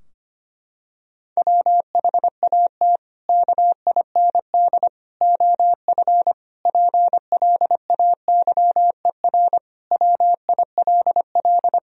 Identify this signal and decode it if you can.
Morse code, heard as a keyed tone, about 25 wpm, 700 Hz. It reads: WHAT KIND OF PLAYER WILL
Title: Sentences from Top 1000 Words - Review Part 4 - 25wpm